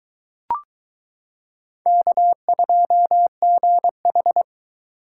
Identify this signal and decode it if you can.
Morse code, heard as a keyed tone, about 22 wpm, 700 Hz. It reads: K2GH